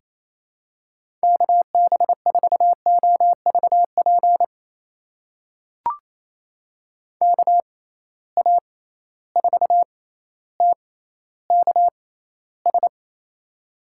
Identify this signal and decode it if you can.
Morse code, heard as a keyed tone, about 28 wpm, spaced as for 14 wpm, 700 Hz. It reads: KB4OVP KA4TKS
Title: USA Call Signs - 6 Characters - 6x Character Spacing - 28wpm